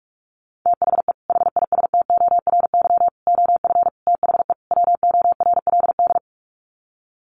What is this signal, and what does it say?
Morse code, heard as a keyed tone, about 44 wpm, 700 Hz. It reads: THE HISTORY OF THE WORLD
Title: Sets of 5 Words Encore2 45wpm